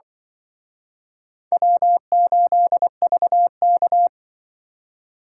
Morse code, heard as a keyed tone, about 24 wpm, 700 Hz. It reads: W8VK
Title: Speed Racing - USA 1x2 Call Signs 20wpm